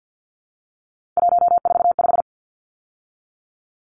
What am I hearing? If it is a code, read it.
Morse code, heard as a keyed tone, about 50 wpm, 700 Hz. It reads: 145
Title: Numbers - 3-Digit 50wpm